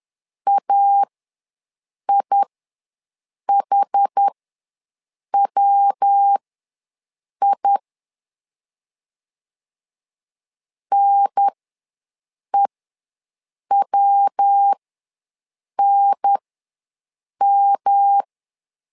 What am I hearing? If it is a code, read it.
Morse code, heard as a keyed tone, about 10 wpm, spaced as for 6 wpm, 800 Hz. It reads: AIHWI NEWNM
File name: LEZIONI / 3 MWRO / 3 MW ws 10wpm 2213